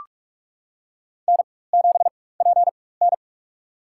Morse code, heard as a keyed tone, about 43 wpm, spaced as for 21 wpm, 700 Hz. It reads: N7PN
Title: USA Call Signs - 4 Characters - 4x Character Spacing - 45wpm